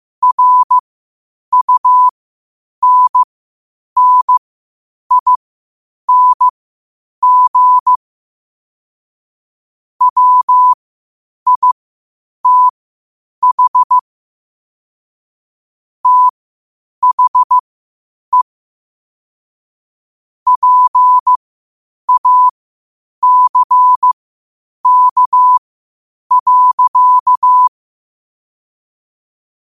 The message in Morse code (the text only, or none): RUNNING WITH THE PACK.